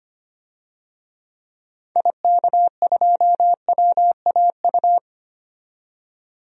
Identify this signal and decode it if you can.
Morse code, heard as a keyed tone, about 25 wpm, 700 Hz. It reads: IK2WAU
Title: DXCC-100 Call Signs - 6 Characters - 25wpm